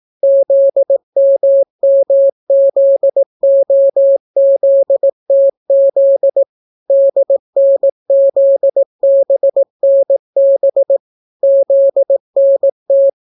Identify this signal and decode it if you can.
Morse code, heard as a keyed tone, about 18 wpm, 550 Hz. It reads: ZMMZOZTZ DNZBNB ZNT